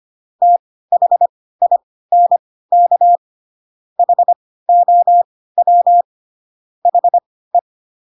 Morse code, heard as a keyed tone, about 25 wpm, spaced as for 16 wpm, 700 Hz. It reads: THINK HOW HE